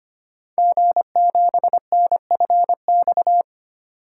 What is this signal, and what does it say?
Morse code, heard as a keyed tone, about 25 wpm, 700 Hz. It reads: G7NFX
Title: Code Groups - 5 Letter:Numbers - 25wpm